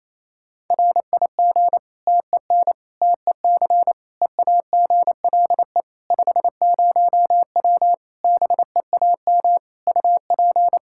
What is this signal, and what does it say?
Morse code, heard as a keyed tone, about 28 wpm, 700 Hz. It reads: RIG TEN TEC EAGLE 50W BEAM UP
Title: Practice QSO - Exchange 2 - Info - 28wpm